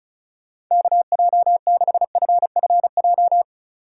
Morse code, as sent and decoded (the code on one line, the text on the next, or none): -.- .--- -.... ..-. ..-. .---
KJ6FFJ